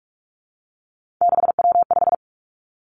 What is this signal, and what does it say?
Morse code, heard as a keyed tone, about 46 wpm, 700 Hz. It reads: 6W5
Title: Three Challenging Letter-Number Combinations 45wpm